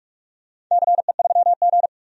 Morse code, heard as a keyed tone, about 46 wpm, 700 Hz. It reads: CE3G
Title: DXCC-100 Call Signs - 4 Characters - 45wpm